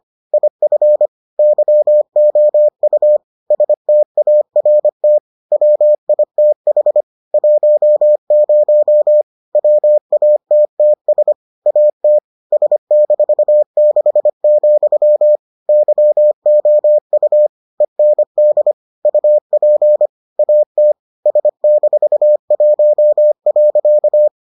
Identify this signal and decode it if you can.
Morse code, heard as a keyed tone, about 25 wpm, 600 Hz. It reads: IF YOU START WITH 10 WATTS AT S-6, YOU END UP AT S-1.